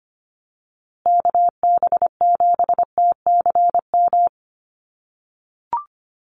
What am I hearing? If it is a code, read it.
Morse code, heard as a keyed tone, about 25 wpm, 700 Hz. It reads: KB7TCM
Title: USA Call Signs Repeated 2x 25wpm